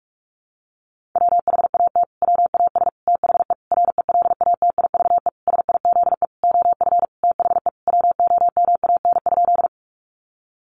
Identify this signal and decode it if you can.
Morse code, heard as a keyed tone, about 44 wpm, 700 Hz. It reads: WHAT WAS THE RELATIVE SIZE OF THE WOMAN?
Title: Sentences from Top 1400 Words 45wpm